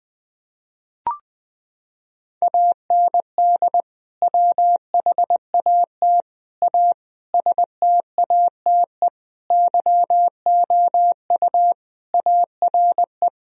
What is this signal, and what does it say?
Morse code, heard as a keyed tone, about 20 wpm, 700 Hz. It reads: AND WHAT A STATE YOU ARE
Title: Sentences from Top 1000 Words - Review Part 2 - 20wpm